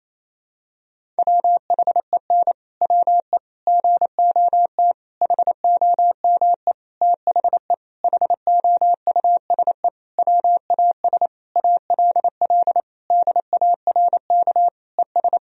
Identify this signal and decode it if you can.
Morse code, heard as a keyed tone, about 28 wpm, 700 Hz. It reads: WHEN WE GOT HOME THE HOUSE WAS ALL DARK ES
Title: Sentences from Top 500 Words and Abbreviations 28wpm